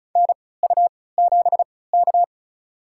Morse code, heard as a keyed tone, about 35 wpm, spaced as for 20 wpm, 700 Hz. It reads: NU7K
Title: USA Call Signs - 3x Character Spacing - 35wpm